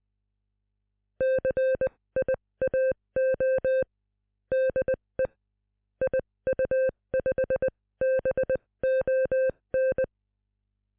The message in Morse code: CIAO DE IU5BON